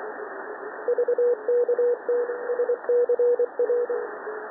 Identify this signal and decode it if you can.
Morse code, heard as a keyed tone, about 23 wpm, 480 Hz, with radio noise. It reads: VKT SCR
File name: TN5E wkg VK7CW 20CW